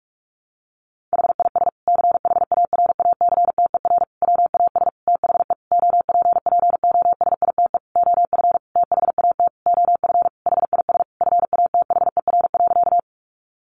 Morse code, heard as a keyed tone, about 44 wpm, 700 Hz. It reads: HIS CHARACTER WAS THE OPPOSITE OF THAT OF HIS FATHER.
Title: Sentences from Top 1300 Words 45wpm